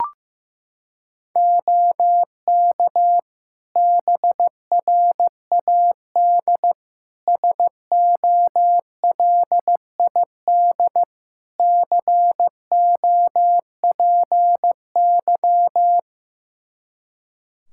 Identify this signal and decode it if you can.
Morse code, heard as a keyed tone, about 15 wpm, 700 Hz. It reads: OK BRAD SOLID COPY
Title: K1USN SST QSO Practice 15wpm